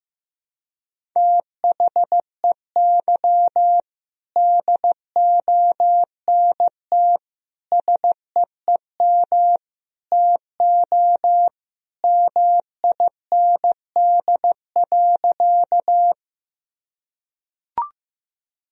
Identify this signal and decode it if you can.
Morse code, heard as a keyed tone, about 15 wpm, 700 Hz. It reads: THEY DONT SEEM TO MIND.
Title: Sentences from Top 1000 Words - Review Part 1 - 15wpm